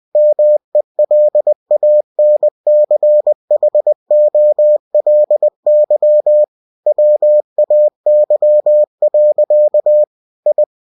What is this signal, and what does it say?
Morse code, heard as a keyed tone, about 20 wpm, 600 Hz. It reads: MELANCHOLY WAY. I
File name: cw-books / winnie-the-pooh / winnie-20wpm-page-0084